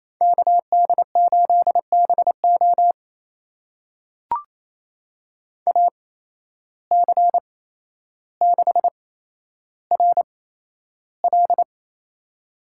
Morse code, heard as a keyed tone, about 28 wpm, spaced as for 12 wpm, 700 Hz. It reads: KD8BO AC6RL
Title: USA Call Signs - 5 Characters - 8x Character Spacing - 28wpm